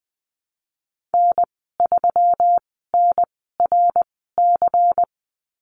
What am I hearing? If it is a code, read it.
Morse code, heard as a keyed tone, about 20 wpm, spaced as for 15 wpm, 700 Hz. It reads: N3NRC
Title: 122 Lesson 20g - Number 3 - Call Signs 20wpm